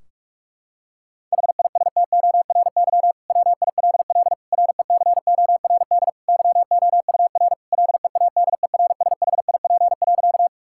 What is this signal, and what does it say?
Morse code, heard as a keyed tone, about 43 wpm, 700 Hz. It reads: HISTORY WILL RECORD YOUR LEADERSHIP.
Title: Sentences from Top 1600 Words 45wpm